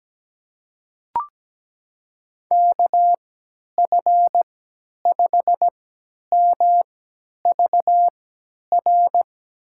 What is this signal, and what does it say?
Morse code, heard as a keyed tone, about 17 wpm, spaced as for 10 wpm, 700 Hz. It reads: KF5MVR
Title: USA Call Signs - 6 Characters - 3x Character Spacing - 17wpm